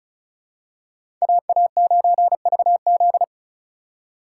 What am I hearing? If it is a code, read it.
Morse code, heard as a keyed tone, about 36 wpm, 700 Hz. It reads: AA9VZ
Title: USA Call Signs - 8x Character Spacing - 35wpm